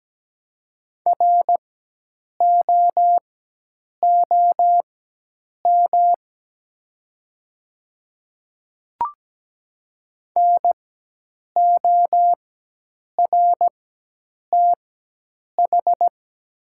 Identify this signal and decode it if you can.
Morse code, heard as a keyed tone, about 17 wpm, spaced as for 8 wpm, 700 Hz. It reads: ROOM NORTH
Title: 4x Character Spacing - Top 300 Words - 17wpm